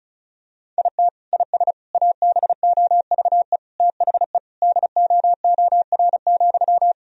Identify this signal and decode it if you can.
Morse code, heard as a keyed tone, about 34 wpm, 700 Hz. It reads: IT IS ABOVE THE DOOR,
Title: Sentences from Top 1000 Words - Review Part 2 - 35wpm